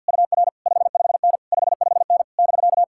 Morse code, heard as a keyed tone, about 62 wpm, 700 Hz. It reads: UR 55N 55N <BK>
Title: Speed-Racing Common QSO Phrases - 50wpm